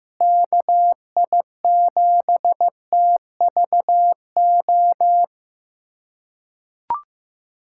Morse code, heard as a keyed tone, about 15 wpm, 700 Hz. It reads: KI7TVO
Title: USA Callsign DE Callsign - 6 Characters - 15wpm